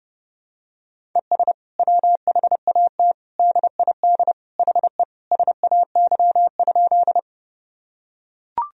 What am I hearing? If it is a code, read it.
Morse code, heard as a keyed tone, about 30 wpm, 700 Hz. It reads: ES WHAT DID HE SAY?
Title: Sentences from Top 200 Words and Abbreviations 30wpm